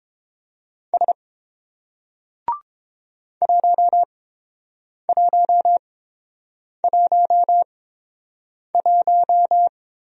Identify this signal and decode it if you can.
Morse code, heard as a keyed tone, about 30 wpm, spaced as for 7 wpm, 700 Hz. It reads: S 1111